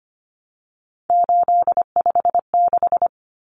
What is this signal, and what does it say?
Morse code, heard as a keyed tone, about 25 wpm, 700 Hz. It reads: 856
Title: Numbers - 3-Digit 25wpm